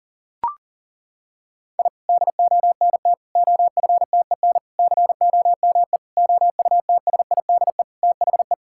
Morse code, heard as a keyed tone, about 37 wpm, 700 Hz. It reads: I DONT OFTEN COME OUTSIDE THE
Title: Sentences from Top 1000 Words - Review Part 6 - 40wpm